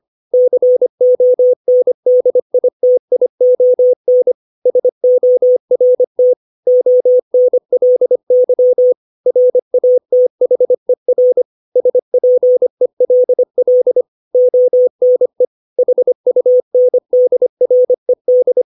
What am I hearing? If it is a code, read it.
Morse code, heard as a keyed tone, about 25 wpm, 500 Hz. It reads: CONDITION SORT ONLY RATHER SPELL ONE HUNDRED
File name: 632 words / 632 Words 25WPM